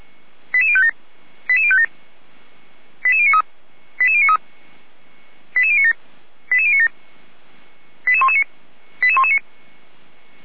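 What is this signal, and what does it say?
Five-tone selective call (ZVEI), heard as quick runs of five tones, tones of about 70 ms. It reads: 88057 88057 88093 88093 88097 88097 88119 88119
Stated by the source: ZVEI five-tone selective call 88053 received by SDR at Feuerwehr Orlamünde.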